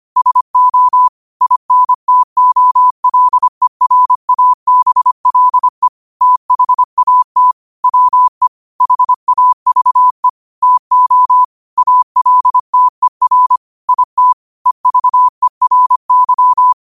Morse code, heard as a keyed tone, about 25 wpm, 1000 Hz. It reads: SO INTOLERABLE THAT WE HAVE TO ALTER IT EVERY
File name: qotdfiles / 25wpm